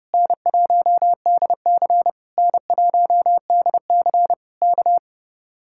Morse code, heard as a keyed tone, about 30 wpm, 700 Hz. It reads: N1DC N1DC K